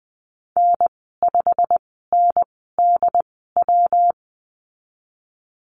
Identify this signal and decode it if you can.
Morse code, heard as a keyed tone, about 19 wpm, spaced as for 14 wpm, 700 Hz. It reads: N5NDW
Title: 122 Lesson 20g - Number 3 - Call Signs 20wpm